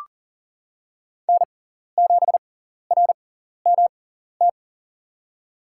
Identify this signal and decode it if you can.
Morse code, heard as a keyed tone, about 40 wpm, spaced as for 14 wpm, 700 Hz. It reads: N7RMT
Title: USA Call Signs - 5 Characters - 6x Character Spacing - 40wpm